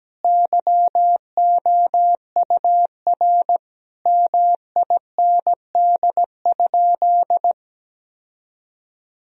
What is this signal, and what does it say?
Morse code, heard as a keyed tone, about 17 wpm, 700 Hz. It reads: YOUR MIND?
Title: Sentences from Top 1000 Words - Review Part 2 - 17wpm